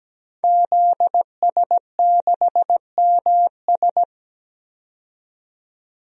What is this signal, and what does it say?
Morse code, heard as a keyed tone, about 17 wpm, 700 Hz. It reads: ZS6MS
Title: DXCC-100 Call Signs - 5 Characters - 17wpm